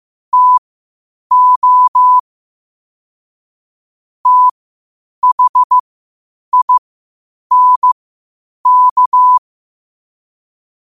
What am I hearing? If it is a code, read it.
Morse code, heard as a keyed tone, about 15 wpm, spaced as for 8 wpm, 1000 Hz. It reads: TO THINK
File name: 8wpm